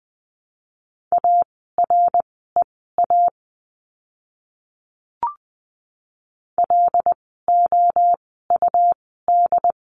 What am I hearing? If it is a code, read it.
Morse code, heard as a keyed tone, about 20 wpm, spaced as for 15 wpm, 700 Hz. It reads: AREA LOUD
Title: 101 Lesson 17f - Letter u - Words 20wpm